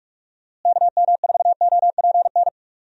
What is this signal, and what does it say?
Morse code, heard as a keyed tone, about 46 wpm, 700 Hz. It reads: KM4OPN
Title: USA Call Signs - 6 Characters - Repeated 2x - 45wpm